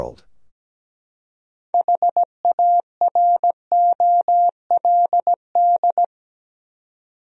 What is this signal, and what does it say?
Morse code, heard as a keyed tone, about 17 wpm, 700 Hz. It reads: HAROLD
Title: USA Common Names - 17wpm